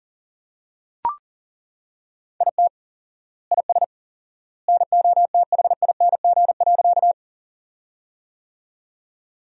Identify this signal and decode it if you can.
Morse code, heard as a keyed tone, about 40 wpm, 700 Hz. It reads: IT IS NOTHING.
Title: 4x Word-Spacing - Sentences from Top 500 Words - 40wpm